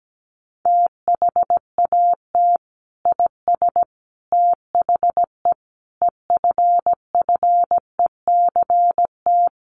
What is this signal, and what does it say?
Morse code, heard as a keyed tone, about 17 wpm, 700 Hz. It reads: THAT IS THE EFFECT
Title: Sentences from Top 1300 Words 17wpm